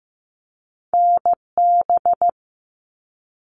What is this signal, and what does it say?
Morse code, heard as a keyed tone, about 15 wpm, 700 Hz. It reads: NB
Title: Canadian Provinces and Territories 15wpm